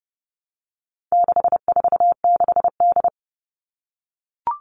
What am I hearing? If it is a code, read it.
Morse code, heard as a keyed tone, about 30 wpm, 700 Hz. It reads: 646D
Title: All Cut Numbers - 4-Digit 30wpm